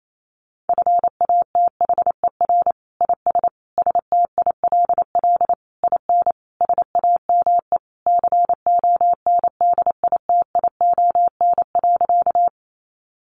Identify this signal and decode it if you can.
Morse code, heard as a keyed tone, about 28 wpm, 700 Hz. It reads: FATHER IS STILL IN SAME CONDITION.
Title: Sentences from Top 1000 Words 28wpm